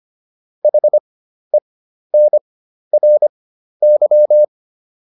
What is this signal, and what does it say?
Morse code, heard as a keyed tone, about 25 wpm, spaced as for 12 wpm, 600 Hz. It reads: HENRY